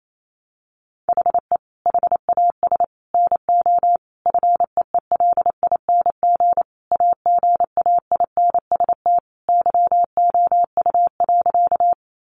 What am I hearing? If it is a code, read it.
Morse code, heard as a keyed tone, about 28 wpm, 700 Hz. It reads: HE HAS NO FEELING AGAINST YOU.
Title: Sentences from Top 1200 Words 28wpm